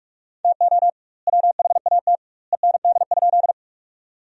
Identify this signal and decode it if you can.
Morse code, heard as a keyed tone, about 46 wpm, 700 Hz. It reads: TO WHAT END?
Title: 2x Word-Spacing - Sentences from Top 200 Words - 45wpm